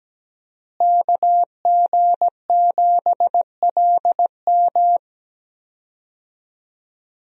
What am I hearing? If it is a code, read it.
Morse code, heard as a keyed tone, about 17 wpm, 700 Hz. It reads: KG7LM
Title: USA Call Signs - 5 Characters - Repeated 2x - 17wpm